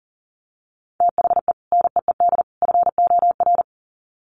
Morse code, heard as a keyed tone, about 40 wpm, 700 Hz. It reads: THE NEED FOR
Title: Sets of 3 Words 40wpm